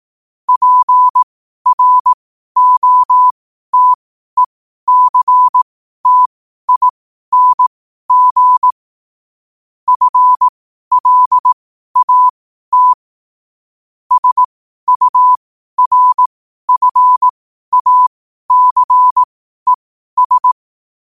Morse code, heard as a keyed tone, about 18 wpm, spaced as for 12 wpm, 1000 Hz. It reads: PROTECTING FLAT SURFACES
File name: qotdfiles / 12wpm